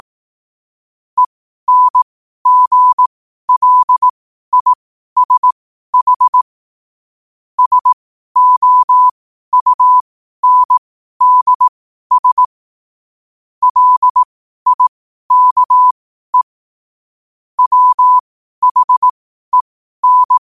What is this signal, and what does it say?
Morse code, heard as a keyed tone, about 18 wpm, spaced as for 12 wpm, 1000 Hz. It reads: ENGLISH SOUNDS LIKE WHEN
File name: qotdfiles / 12wpm